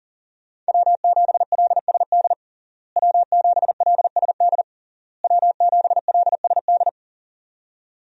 Morse code, heard as a keyed tone, about 40 wpm, 700 Hz. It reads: W7LSD W7LSD W7LSD
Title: USA Call Signs - 5 Characters - Repeated 3x - 40wpm